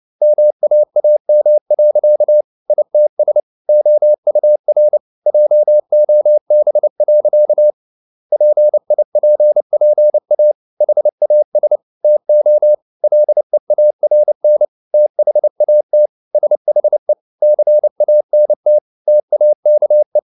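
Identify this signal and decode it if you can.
Morse code, heard as a keyed tone, about 29 wpm, 600 Hz. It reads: MAAM. ITS OUR JOB. PIPPA HAS TO LEARN THAT SHE CANT TAKE
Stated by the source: Morse code practice at 28 wpm, text of Ken Follett's novel Never in English, from CW med Gnister.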